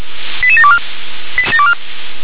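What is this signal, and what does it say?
Five-tone selective call (ZVEI), heard as quick runs of five tones, tones of about 80 ms. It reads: 88724
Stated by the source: ZVEI five-tone alert received by SDR.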